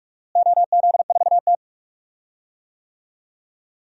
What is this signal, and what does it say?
Morse code, heard as a keyed tone, about 46 wpm, 700 Hz. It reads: OZ4T
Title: DXCC-100 Call Signs - 4 Characters - Repeated 2x - 45wpm